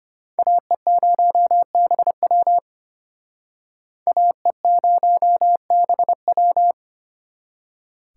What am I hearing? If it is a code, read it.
Morse code, heard as a keyed tone, about 33 wpm, 700 Hz. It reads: AE0BW AE0BW